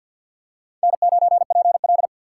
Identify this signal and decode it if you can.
Morse code, heard as a keyed tone, about 53 wpm, 700 Hz. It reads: N9PL
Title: USA Call Signs - 4 Characters - 6x Character Spacing - 50wpm